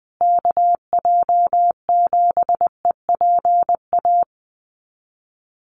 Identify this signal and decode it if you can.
Morse code, heard as a keyed tone, about 20 wpm, 700 Hz. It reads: KJ7EPA
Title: USA Call Signs Repeated 2x 20wpm